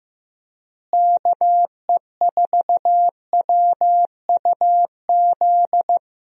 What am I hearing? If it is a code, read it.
Morse code, heard as a keyed tone, about 16 wpm, 700 Hz. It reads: KE4WUZ
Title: USA Call Signs - 6 Characters - Repeated 3x - 15wpm